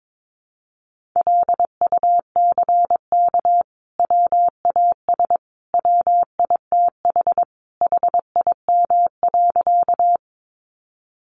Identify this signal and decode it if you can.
Morse code, heard as a keyed tone, about 22 wpm, 700 Hz. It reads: LUCK WAS WITH HIM.
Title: Sentences from Top 1500 Words 22wpm